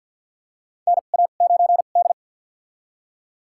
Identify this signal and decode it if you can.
Morse code, heard as a keyed tone, about 52 wpm, spaced as for 37 wpm, 700 Hz. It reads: NA9D
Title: USA Call Signs - 4 Characters - 2x Character Spacing - 50wpm